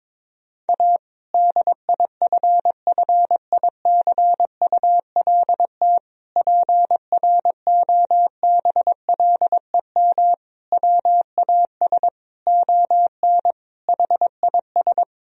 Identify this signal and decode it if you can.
Morse code, heard as a keyed tone, about 22 wpm, 700 Hz. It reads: A DIFFICULT PROBLEM WAS ON HIS
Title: Sentences from Top 1000 Words - Review Part 2 - 22wpm